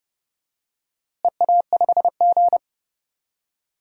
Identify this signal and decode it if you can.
Morse code, heard as a keyed tone, about 30 wpm, 700 Hz. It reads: EA5G
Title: DXCC-100 Call Signs - 4 Characters - Repeated 2x - 30wpm